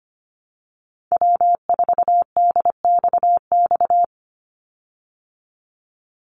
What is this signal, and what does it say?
Morse code, heard as a keyed tone, about 25 wpm, 700 Hz. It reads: W4DXX
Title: Call Signs Repeated 2x - 25wpm